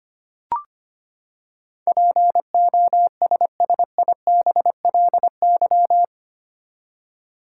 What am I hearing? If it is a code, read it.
Morse code, heard as a keyed tone, about 25 wpm, 700 Hz. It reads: POSSIBLY
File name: Eight Letter Words - 25wpm